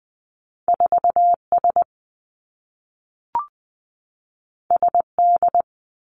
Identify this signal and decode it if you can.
Morse code, heard as a keyed tone, about 20 wpm, 700 Hz. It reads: 4 S S D
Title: Two Challenging Letter-Number Combinations 20wpm